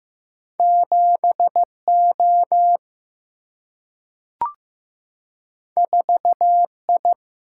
Morse code, heard as a keyed tone, about 15 wpm, 700 Hz. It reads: 7 O 4 I
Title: Single-Letter Single-Number 15wpm